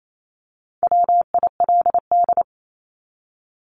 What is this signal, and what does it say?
Morse code, heard as a keyed tone, about 28 wpm, 700 Hz. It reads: WILD
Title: Top 1501-1750 Words 28wpm